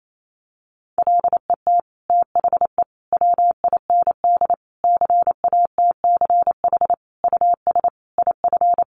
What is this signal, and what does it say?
Morse code, heard as a keyed tone, about 28 wpm, 700 Hz. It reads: LET THE WIND CATCH US IF
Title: Sentences from Top 1500 Words 28wpm